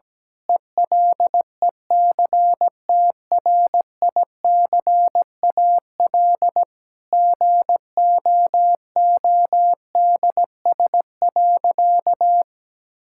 Morse code, heard as a keyed tone, about 17 wpm, 700 Hz. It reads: ELECTRICAL GOODS.
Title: Sentences from Top 1000 Words - Review Part 6 - 17wpm